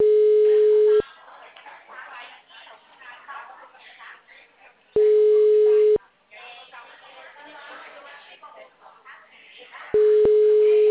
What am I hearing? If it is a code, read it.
Ringback tone (Europe, 425 Hz), with speech.
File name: out- -2003- - - .75